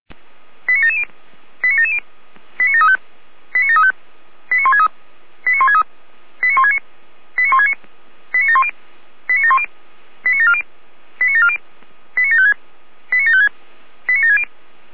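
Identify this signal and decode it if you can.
Five-tone selective call (ZVEI), heard as quick runs of five tones, tones of about 80 ms. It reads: 89770 89770 89735 89735 89163 89163 89169 89169 89710 89710 89740 89740 89756 89756 89760
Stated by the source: Five-tone ZVEI call received on an SDR.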